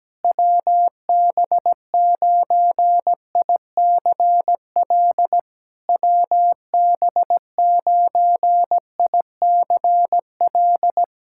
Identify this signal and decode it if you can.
Morse code, heard as a keyed tone, about 17 wpm, 700 Hz. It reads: WB9ICL WB9ICL
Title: SOTA QSO Practice 17wpm